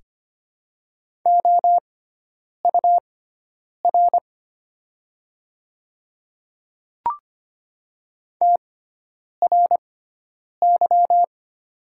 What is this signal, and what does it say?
Morse code, heard as a keyed tone, about 25 wpm, spaced as for 9 wpm, 700 Hz. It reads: OUR TRY
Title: 6x Character Spacing - Top 300 Words - 25wpm